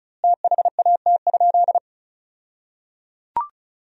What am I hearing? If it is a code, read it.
Morse code, heard as a keyed tone, about 36 wpm, 700 Hz. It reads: THAT?
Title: 2x Word-Spacing - Sentences from Top 300 Words - 35wpm